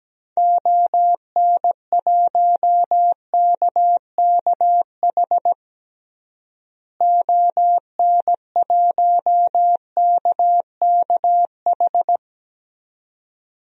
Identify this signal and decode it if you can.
Morse code, heard as a keyed tone, about 17 wpm, 700 Hz. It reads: ON1KKH ON1KKH
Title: DXCC-100 Call Signs - 6 Characters - Repeated 2x - 17wpm